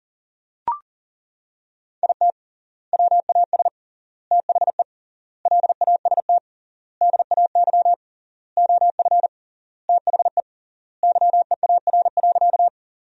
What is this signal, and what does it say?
Morse code, heard as a keyed tone, about 40 wpm, 700 Hz. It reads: IT WAS THE LAST DAY OF THE YEAR.
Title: 3x Word-Spacing - Sentences from Top 300 Words - 40wpm